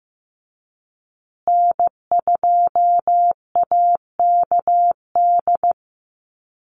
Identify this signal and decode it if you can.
Morse code, heard as a keyed tone, about 15 wpm, 700 Hz. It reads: N2AKD
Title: USA Call Signs Repeated 3x 15wpm